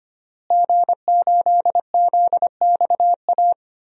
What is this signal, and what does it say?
Morse code, heard as a keyed tone, about 25 wpm, 700 Hz. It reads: G8ZXA
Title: DXCC-100 Call Signs - 5 Characters - Repeated 3x - 25wpm